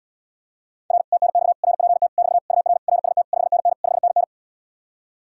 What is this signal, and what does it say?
Morse code, heard as a keyed tone, about 79 wpm, 700 Hz. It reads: R TNX RPT <BT> UR RST 5NN 5NN